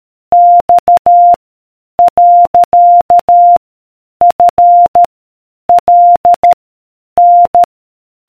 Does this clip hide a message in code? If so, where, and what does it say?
Yes: Morse code from 0.3 to 7.6 s, heard as a keyed tone, about 13 wpm, 700 Hz. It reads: X . F L N